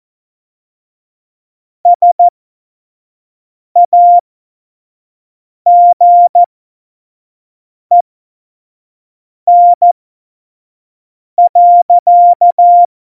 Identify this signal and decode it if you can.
Morse code, heard as a keyed tone, about 14 wpm, spaced as for 5 wpm, 700 Hz. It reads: SAGEN.
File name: Morsekurs / Witze / Witz 13. August 2019, 25 F70